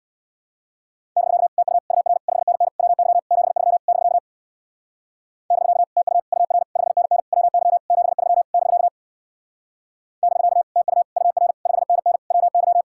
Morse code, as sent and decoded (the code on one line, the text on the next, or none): -...-.- / - ..- / ..- .-. / ..... -. -. / .-- -.-- / --... ...-- / -...-.- / -...-.- / - ..- / ..- .-. / ..... -. -. / .-- -.-- / --... ...-- / -...-.- / -...-.- / - ..- / ..- .-. / ..... -. -. / .-- -.--
<BK> TU UR 5NN WY 73 <BK> <BK> TU UR 5NN WY 73 <BK> <BK> TU UR 5NN WY